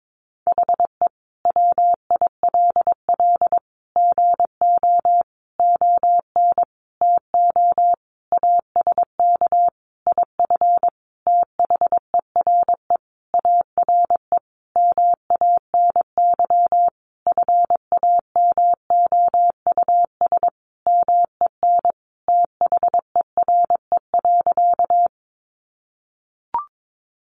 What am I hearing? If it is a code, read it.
Morse code, heard as a keyed tone, about 22 wpm, 700 Hz. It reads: HE WILL GO ON TO ASK IF THERE ARE MANY FAMOUS MEN THERE.
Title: Sentences from Top 1500 Words 22wpm